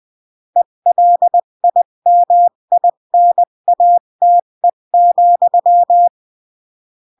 Morse code, heard as a keyed tone, about 20 wpm, 700 Hz. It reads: ELIMINATE,
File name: CW / jodorowsky / 82 commandments 20-15